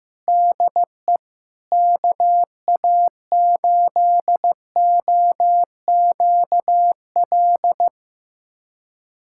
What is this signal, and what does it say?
Morse code, heard as a keyed tone, about 15 wpm, 700 Hz. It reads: DE KA8OQL
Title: USA Callsign DE Callsign 15wpm